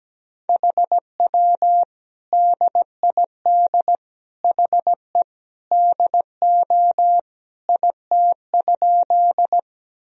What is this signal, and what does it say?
Morse code, heard as a keyed tone, about 17 wpm, 700 Hz. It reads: HW DID HE DO IT?
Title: Sentences from Top 100 Words and Abbreviations 17wpm